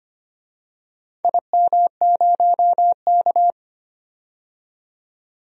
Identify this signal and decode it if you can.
Morse code, heard as a keyed tone, about 25 wpm, 700 Hz. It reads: IM0K
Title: DXCC-100 Call Signs - 4 Characters - 25wpm